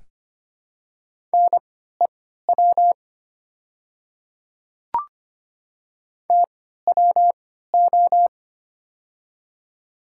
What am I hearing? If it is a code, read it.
Morse code, heard as a keyed tone, about 25 wpm, spaced as for 14 wpm, 700 Hz. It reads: NEW TWO
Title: 3x Character Spacing - Top 200 Words - 25wpm